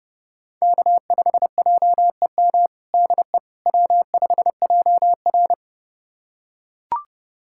Morse code, heard as a keyed tone, about 30 wpm, 700 Hz. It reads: K5JEM DE W5JR
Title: USA Callsign DE Callsign 30wpm